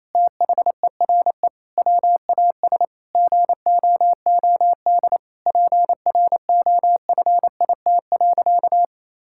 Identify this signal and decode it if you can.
Morse code, heard as a keyed tone, about 27 wpm, 700 Hz. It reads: THERE WAS GOOD PROFIT.
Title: Sentences from Top 1600 Words 28wpm